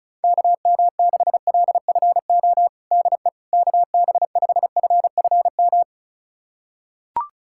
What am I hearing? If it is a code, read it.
Morse code, heard as a keyed tone, about 36 wpm, 700 Hz. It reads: KM6LFO DE KB5FFM
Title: USA Callsign DE Callsign - 6 Characters - 35wpm